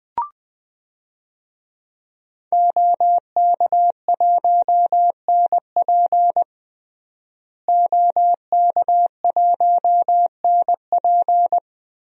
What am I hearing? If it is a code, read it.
Morse code, heard as a keyed tone, about 21 wpm, 700 Hz. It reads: OK1NP OK1NP
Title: DXCC-100 Call Signs - 5 Characters - Repeated 3x - 20wpm